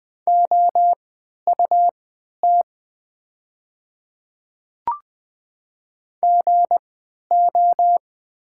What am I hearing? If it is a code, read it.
Morse code, heard as a keyed tone, about 20 wpm, spaced as for 11 wpm, 700 Hz. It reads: OUT GO
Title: 3x Character Spacing - Top 100 Words - 20wpm